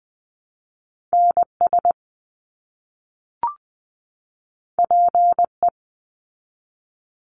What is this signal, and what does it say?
Morse code, heard as a keyed tone, about 20 wpm, 700 Hz. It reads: NS PE
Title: Canadian Provinces and Territories 20wpm